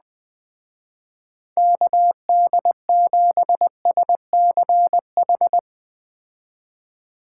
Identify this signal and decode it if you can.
Morse code, heard as a keyed tone, about 20 wpm, 700 Hz. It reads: KD7SCH
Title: Speed Racing - USA 2x3 Call Signs 17wpm